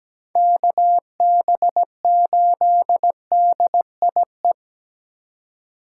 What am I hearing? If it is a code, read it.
Morse code, heard as a keyed tone, about 17 wpm, 700 Hz. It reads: KB8DIE
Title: USA Call Signs - 6 Characters - 8x Character Spacing - 17wpm